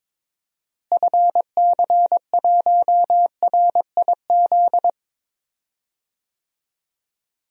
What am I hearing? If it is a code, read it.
Morse code, heard as a keyed tone, about 22 wpm, 700 Hz. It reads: FC1RIZ
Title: DXCC-100 Call Signs - 6 Characters - Repeated 2x - 22wpm